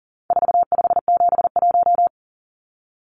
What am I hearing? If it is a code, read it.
Morse code, heard as a keyed tone, about 40 wpm, 700 Hz. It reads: 4571
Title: Numbers - 4-Digit 40wpm